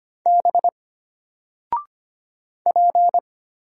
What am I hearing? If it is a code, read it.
Morse code, heard as a keyed tone, about 25 wpm, 700 Hz. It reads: B P
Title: Single Character (Letter:Number:Punctuation:Prosign) Alt 25wpm